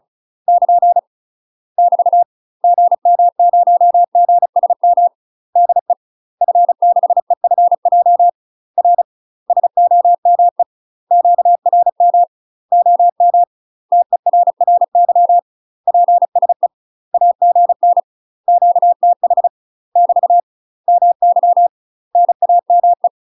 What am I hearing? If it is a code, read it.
Morse code, heard as a keyed tone, about 36 wpm, 700 Hz. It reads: <KN> <BT> GM0GSM DE F6EFJ R SOME QRM OM TERRY PSE AGN QTH <BT> MY NAME
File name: UK MASTER QSOs TRAINING/35WPM